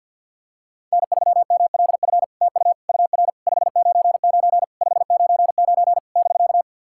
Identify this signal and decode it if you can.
Morse code, heard as a keyed tone, about 48 wpm, 700 Hz. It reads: N3MLF TU UR 599 599 <BK>
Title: SOTA QSO Practice 50wpm